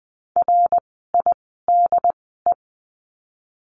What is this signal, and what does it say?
Morse code, heard as a keyed tone, about 20 wpm, spaced as for 15 wpm, 700 Hz. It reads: RIDE
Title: 068 Lesson 12f - Letter d - Words 20wpm